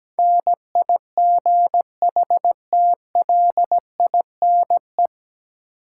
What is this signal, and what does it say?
Morse code, heard as a keyed tone, about 17 wpm, 700 Hz. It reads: NIGHTLINE
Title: Sets of 5 Words - New Word Drill - 17wpm